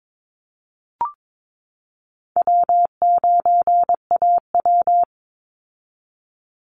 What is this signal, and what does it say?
Morse code, heard as a keyed tone, about 22 wpm, 700 Hz. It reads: W9AW
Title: Call Signs - 22wpm